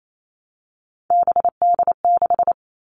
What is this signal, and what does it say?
Morse code, heard as a keyed tone, about 28 wpm, 700 Hz. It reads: BD6
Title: All Cut Numbers - 3-Digit 28wpm